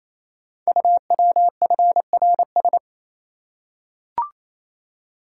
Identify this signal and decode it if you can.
Morse code, heard as a keyed tone, about 29 wpm, 700 Hz. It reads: UWFRS
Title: Code Groups - 5 Letters - 28wpm